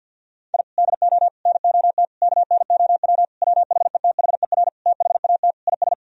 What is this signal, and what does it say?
Morse code, heard as a keyed tone, about 50 wpm, 700 Hz. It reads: I DO NOT KNOW WHETHER THAT IS
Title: Sentences from Top 1000 Words - Review Part 3 - 50wpm